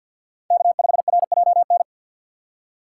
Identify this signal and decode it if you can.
Morse code, heard as a keyed tone, about 53 wpm, 700 Hz. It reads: K5RJN
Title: USA Call Signs - 5 Characters - 4x Character Spacing - 50wpm